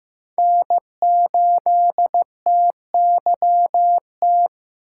Morse code, heard as a keyed tone, about 15 wpm, 700 Hz. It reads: N8TYT
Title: USA Call Signs - 5 Characters - 8x Character Spacing - 15wpm